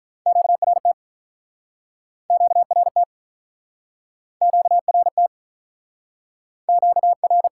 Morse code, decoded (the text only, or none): QRT QRT QRT QR